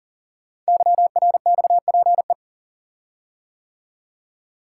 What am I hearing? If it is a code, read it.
Morse code, heard as a keyed tone, about 38 wpm, 690 Hz. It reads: YRXPE
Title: Code Groups - 5 Letters - 40wpm